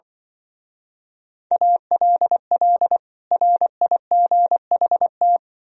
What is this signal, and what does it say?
Morse code, heard as a keyed tone, about 24 wpm, 700 Hz. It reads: ALL RIGHT
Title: Speed-Racing - Sets of 2 Words 20wpm